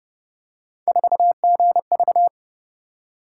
Morse code, heard as a keyed tone, about 30 wpm, 700 Hz. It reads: 4GV
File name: Speed-Racing - Three Letter-Number Combinations - 25wpm